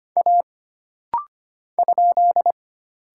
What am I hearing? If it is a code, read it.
Morse code, heard as a keyed tone, about 25 wpm, 700 Hz. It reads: A ?
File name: Single Character (Letter:Number:Punctuation:Prosign) - Rapid-Fire - 25wpm